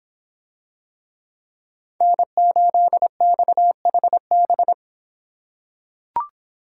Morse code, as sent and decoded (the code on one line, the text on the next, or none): -. ---.. -..- .... -...
N8XHB